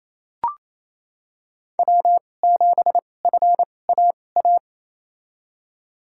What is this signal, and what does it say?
Morse code, heard as a keyed tone, about 28 wpm, spaced as for 20 wpm, 700 Hz. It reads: W7FAA
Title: USA Call Signs - 5 Characters - 2x Character Spacing - 28wpm